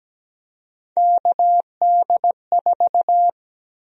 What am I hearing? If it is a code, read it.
Morse code, heard as a keyed tone, about 17 wpm, 700 Hz. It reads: KD4
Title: USA Call Sign Prefixes - 17wpm